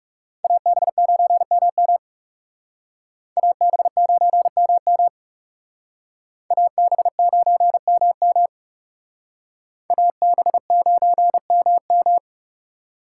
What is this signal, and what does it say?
Morse code, heard as a keyed tone, about 38 wpm, 700 Hz. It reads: AB9MM AB9MM AB9MM AB9MM